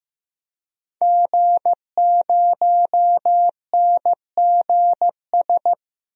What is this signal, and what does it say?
Morse code, heard as a keyed tone, about 16 wpm, 700 Hz. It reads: G0NGS
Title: DXCC-100 Call Signs - 5 Characters - 15wpm